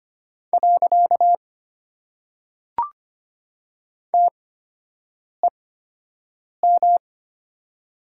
Morse code, heard as a keyed tone, about 25 wpm, 700 Hz. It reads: . T E M
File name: Top 90 QSO Elements - 8x Character Spacing - 25wpm